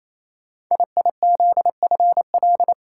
Morse code, heard as a keyed tone, about 28 wpm, 700 Hz. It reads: IIZFL
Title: Code Groups - 5 Letters - 28wpm